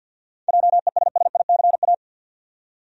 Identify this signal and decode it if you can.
Morse code, heard as a keyed tone, about 52 wpm, 700 Hz. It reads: JESSICA
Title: USA Common Names - 50wpm